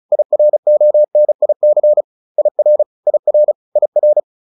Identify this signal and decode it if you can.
Morse code, heard as a keyed tone, about 34 wpm, 600 Hz. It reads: IRONIC IR IR IR